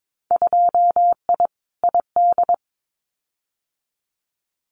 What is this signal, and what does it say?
Morse code, heard as a keyed tone, about 22 wpm, 700 Hz. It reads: 2I ID
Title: Winter Field Day Exchanges Encore 22wpm